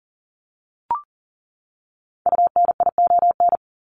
Morse code, heard as a keyed tone, about 40 wpm, 700 Hz. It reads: UNION
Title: Top 1251-1500 Words 40wpm